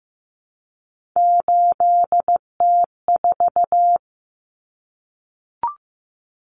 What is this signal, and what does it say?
Morse code, heard as a keyed tone, about 15 wpm, 700 Hz. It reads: 8T4
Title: Common Cut Numbers - 3-Digit 15wpm